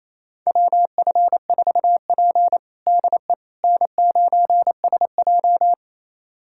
Morse code, heard as a keyed tone, about 28 wpm, 700 Hz. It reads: WF4P DE N9SJ
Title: USA Callsign DE Callsign - 4 Characters - 28wpm